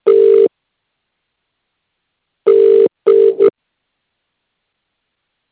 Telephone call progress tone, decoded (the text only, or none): ringback tone